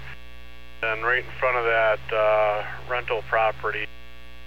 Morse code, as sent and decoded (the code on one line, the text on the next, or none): none